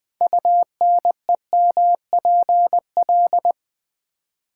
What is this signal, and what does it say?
Morse code, heard as a keyed tone, about 20 wpm, 700 Hz. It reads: UNEMPL
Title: Build A Word - 6 Letters At A Time - 12-Letter Words - 20wpm